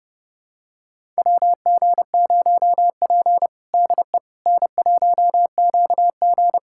Morse code, heard as a keyed tone, about 30 wpm, 700 Hz. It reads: WG0P DE N1QG
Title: USA Callsign DE Callsign - 4 Characters - 30wpm